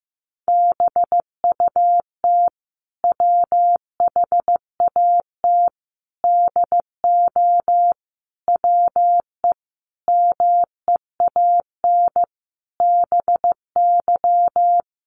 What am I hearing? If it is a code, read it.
Morse code, heard as a keyed tone, about 15 wpm, 700 Hz. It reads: BUT WHAT DO WE MEAN BY
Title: Sentences from Top 1100 Words 15wpm